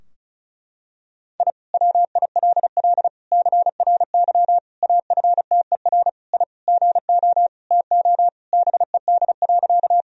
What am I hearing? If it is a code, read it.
Morse code, heard as a keyed tone, about 34 wpm, 700 Hz. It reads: I WILL CRY AFTER I GO TO BED.
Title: Sentences from Top 1000 Words - Review Part 2 - 35wpm